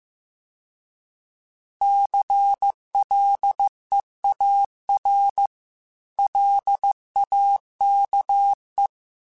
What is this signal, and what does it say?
Morse code, heard as a keyed tone, about 15 wpm, 800 Hz. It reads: CLEAR LAKE